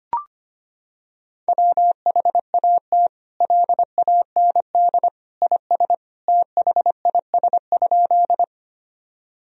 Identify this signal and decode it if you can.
Morse code, heard as a keyed tone, about 25 wpm, 700 Hz. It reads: WHAT LAND IS THIS?
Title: Sentences from Top 1000 Words - Review Part 1 - 25wpm